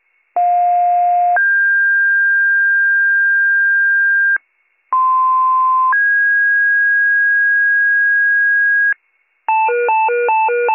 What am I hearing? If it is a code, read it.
Two-tone page, heard as a one-second tone, then a three-second tone, about 710 then 1600 Hz, about 1040 then 1650 Hz. It is heard from 0.4 to 8.9 s.